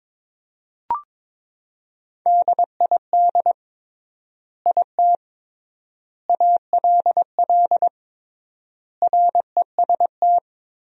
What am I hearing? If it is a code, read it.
Morse code, heard as a keyed tone, about 22 wpm, 700 Hz. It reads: DID IT ALL REST
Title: 3x Word-Spacing - Sentences from Top 500 Words - 22wpm